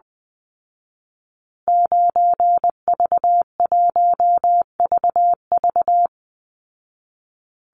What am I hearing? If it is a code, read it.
Morse code, heard as a keyed tone, about 20 wpm, 700 Hz. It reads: 9V1VV
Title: Call Signs Repeated 2x - 20wpm